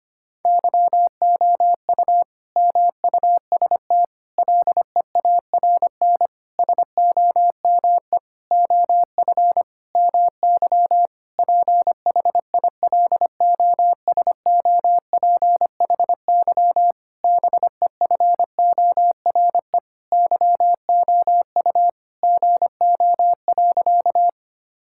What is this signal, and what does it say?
Morse code, heard as a keyed tone, about 25 wpm, 700 Hz. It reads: YOU MUST LEARN SOME OF MY PHILOSOPHY BEFORE YOU GO.
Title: Sentences from Top 1000 Words - Review Part 6 - 25wpm